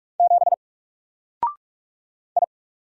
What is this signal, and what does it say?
Morse code, heard as a keyed tone, about 45 wpm, 700 Hz. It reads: 7 I
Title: Single Letter-Number Alt 45wpm